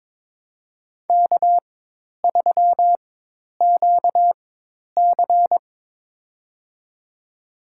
Morse code, heard as a keyed tone, about 22 wpm, spaced as for 10 wpm, 700 Hz. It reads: K3QC
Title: USA Call Signs - 4 Characters - 4x Character Spacing - 22wpm